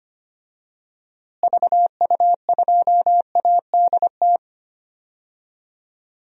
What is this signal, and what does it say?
Morse code, heard as a keyed tone, about 25 wpm, 700 Hz. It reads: VU2ADT
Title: DXCC-100 Call Signs - 6 Characters - Repeated 2x - 25wpm